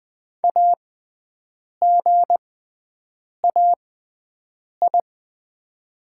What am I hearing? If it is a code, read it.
Morse code, heard as a keyed tone, about 21 wpm, spaced as for 7 wpm, 700 Hz. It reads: AGAI
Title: 6x Character Spacing - Top 200 Words - 20wpm